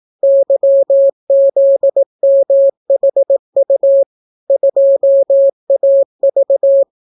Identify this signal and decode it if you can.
Morse code, heard as a keyed tone, about 18 wpm, 550 Hz. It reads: YZMHU 2AV